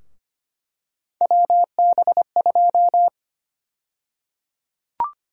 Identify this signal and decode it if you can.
Morse code, heard as a keyed tone, about 25 wpm, 700 Hz. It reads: WB2
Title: USA Call Sign Prefixes - Repeated 2x - 25wpm